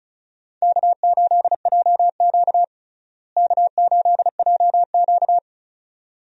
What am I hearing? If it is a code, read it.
Morse code, heard as a keyed tone, about 36 wpm, 700 Hz. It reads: K8JQ K8JQ
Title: USA Call Signs - 4 Characters - Repeated 2x - 35wpm